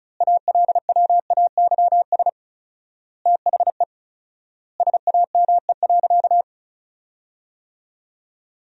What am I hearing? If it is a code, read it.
Morse code, heard as a keyed tone, about 34 wpm, 700 Hz. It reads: ALWAYS THE SAME.
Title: 4x Word-Spacing - Sentences from Top 300 Words - 35wpm